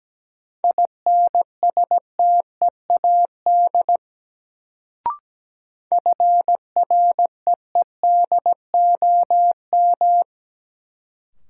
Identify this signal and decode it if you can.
Morse code, heard as a keyed tone, about 17 wpm, 700 Hz. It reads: INSTEAD FREEDOM
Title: Sentences from Top 800 Words - New Word Drill - 17wpm